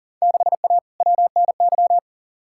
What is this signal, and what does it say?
Morse code, heard as a keyed tone, about 40 wpm, 700 Hz. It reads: 6A WNY
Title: Field Day Exchanges Encore - 40wpm